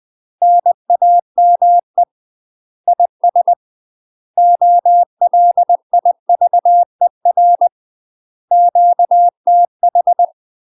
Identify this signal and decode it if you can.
Morse code, heard as a keyed tone, about 20 wpm, 700 Hz. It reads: NAME IS OLIVER QTH
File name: UK MASTER QSOs TRAINING / 20WPM / UK MASTER QSOs 0048